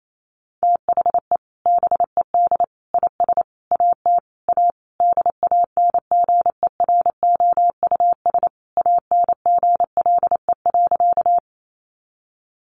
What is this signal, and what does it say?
Morse code, heard as a keyed tone, about 28 wpm, 700 Hz. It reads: THE BED IS AT A DANGEROUS ANGLE.
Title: Sentences from Top 1500 Words 28wpm